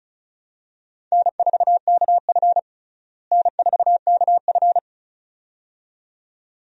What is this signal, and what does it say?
Morse code, heard as a keyed tone, about 36 wpm, 700 Hz. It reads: N4KF N4KF
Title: USA Call Signs - 4 Characters - Repeated 2x - 35wpm